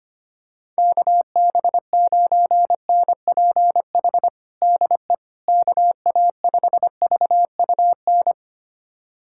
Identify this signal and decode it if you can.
Morse code, heard as a keyed tone, about 25 wpm, 700 Hz. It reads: KB9NPH DE KA5VUN
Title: USA Callsign DE Callsign 25wpm